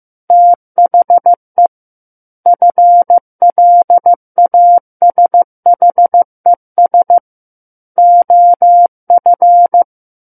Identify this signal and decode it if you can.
Morse code, heard as a keyed tone, about 15 wpm, 700 Hz. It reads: THE FLASHES OF
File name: Chapter 6 - 15 WPM